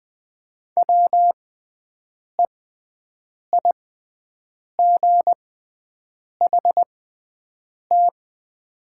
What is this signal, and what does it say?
Morse code, heard as a keyed tone, about 20 wpm, spaced as for 7 wpm, 690 Hz. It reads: WEIGHT
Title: 6x Character Spacing - Top 500 Words - 20wpm